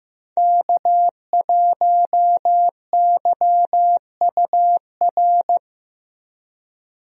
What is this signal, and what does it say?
Morse code, heard as a keyed tone, about 15 wpm, 700 Hz. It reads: K1YUR
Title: USA Call Signs - 5 Characters - Repeated 3x - 15wpm